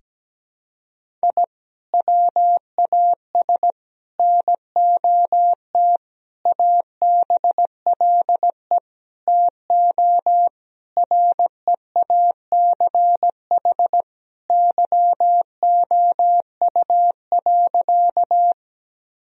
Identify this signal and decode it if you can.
Morse code, heard as a keyed tone, about 17 wpm, 700 Hz. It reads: I WAS NOT ABLE TO REACH YOU.
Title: Sentences from Top 1000 Words - Review Part 2 - 17wpm